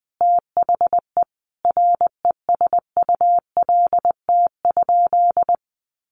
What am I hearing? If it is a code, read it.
Morse code, heard as a keyed tone, about 20 wpm, 700 Hz. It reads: THE RESULT?
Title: Sentences from Top 1500 Words 20wpm